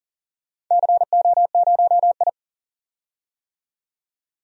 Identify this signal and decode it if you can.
Morse code, heard as a keyed tone, about 40 wpm, 700 Hz. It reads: CO0I
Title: DXCC-100 Call Signs - 4 Characters - 40wpm